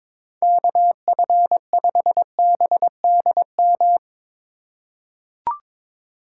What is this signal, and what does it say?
Morse code, heard as a keyed tone, about 22 wpm, 700 Hz. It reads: KF5BDM
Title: USA Call Signs - 6 Characters - 3x Character Spacing - 22wpm